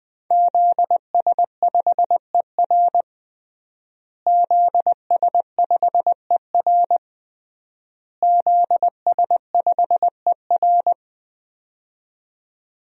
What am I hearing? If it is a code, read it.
Morse code, heard as a keyed tone, about 20 wpm, 700 Hz. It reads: ZS5ER ZS5ER ZS5ER
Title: DXCC-100 Call Signs - 5 Characters - Repeated 3x - 20wpm